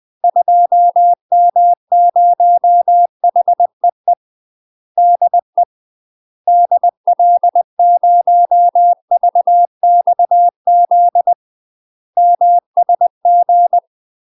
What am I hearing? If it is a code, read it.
Morse code, heard as a keyed tone, about 20 wpm, 700 Hz. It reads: 2M0HEE DE DL0VXZ MSG
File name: UK MASTER QSOs TRAINING / 20WPM / UK MASTER QSOs 0055